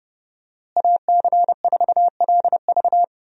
Morse code, heard as a keyed tone, about 30 wpm, 700 Hz. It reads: AC4LV
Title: USA Call Signs - 5 Characters - Repeated 2x - 30wpm